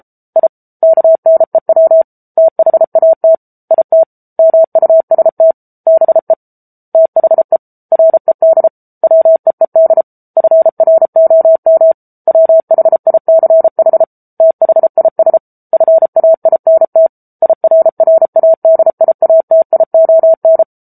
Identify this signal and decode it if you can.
Morse code, heard as a keyed tone, about 34 wpm, 650 Hz. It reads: I KNEW THAT IT MUST BE THE RED WEED FROM WHICH THIS FAINT IRRADIATION
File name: Book Two / 07-Chapter 7 - 33 WPM